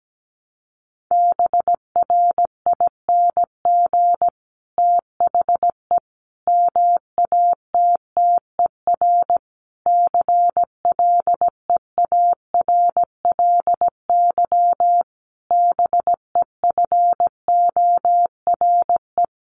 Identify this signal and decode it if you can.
Morse code, heard as a keyed tone, about 17 wpm, 700 Hz. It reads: BRING THE MATTER CLEARLY BEFORE
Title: Sentences from Top 1300 Words 17wpm